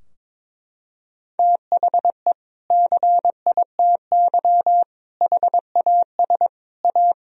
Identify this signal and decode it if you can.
Morse code, heard as a keyed tone, about 22 wpm, 700 Hz. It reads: THE CITY HAS A
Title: Sentences from Top 1000 Words - Review Part 4 - 22wpm